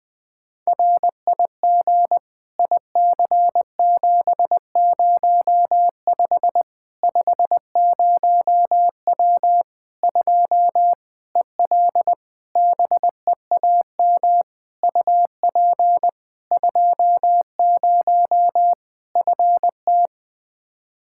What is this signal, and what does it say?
Morse code, heard as a keyed tone, about 21 wpm, 700 Hz. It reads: RIG IC705 50W 2 EL BEAM UP 20 FT
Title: Practice QSO - Exchange 2 - Info - 20wpm